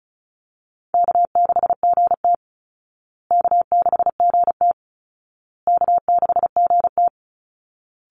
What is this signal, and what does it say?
Morse code, heard as a keyed tone, about 34 wpm, 700 Hz. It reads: K6GT K6GT K6GT
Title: Easy Call Signs Repeated 3x - 35wpm